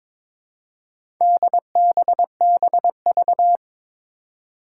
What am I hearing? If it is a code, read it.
Morse code, heard as a keyed tone, about 22 wpm, 700 Hz. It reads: DBBV
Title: Four Challenging Letter-Number Combinations 22wpm